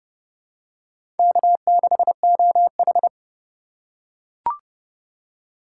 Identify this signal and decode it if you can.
Morse code, heard as a keyed tone, about 29 wpm, 700 Hz. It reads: K6OH